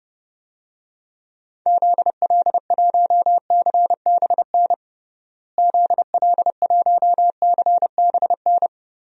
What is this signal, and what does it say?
Morse code, heard as a keyed tone, about 32 wpm, 700 Hz. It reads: ZL1CBN ZL1CBN
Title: DXCC-100 Call Signs - 6 Characters - Repeated 3x - 30wpm